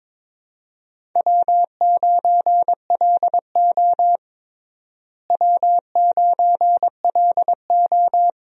Morse code, heard as a keyed tone, about 21 wpm, 700 Hz. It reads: W9LO W9LO
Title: USA Call Signs - 4 Characters - Repeated 2x - 22wpm